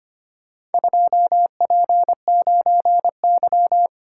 Morse code, heard as a keyed tone, about 25 wpm, 700 Hz. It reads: 2P9Y
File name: practice / 25wpm / Four Letter-Number Combinations - 25wpm